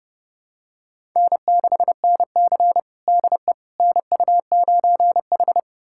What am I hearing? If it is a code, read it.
Morse code, heard as a keyed tone, about 32 wpm, 700 Hz. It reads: N6NC DE NU9H
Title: USA Callsign DE Callsign - 4 Characters - 30wpm